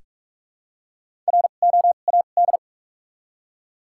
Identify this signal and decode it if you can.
Morse code, heard as a keyed tone, about 42 wpm, spaced as for 31 wpm, 700 Hz. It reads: ROAD
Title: 2x Character Spacing - Top 500 Words - 45wpm